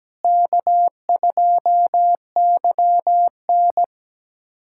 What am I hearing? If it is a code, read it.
Morse code, heard as a keyed tone, about 17 wpm, 700 Hz. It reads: K2YN
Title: USA Callsign DE Callsign - 4 Characters - 17wpm